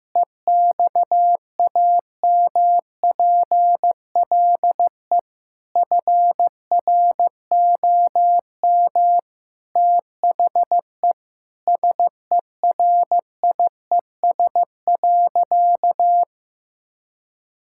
Morse code, heard as a keyed tone, about 15 wpm, 700 Hz. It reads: EXAMPLE FROM THE SERIES.
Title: Sentences from Top 1000 Words - Review Part 5 - 15wpm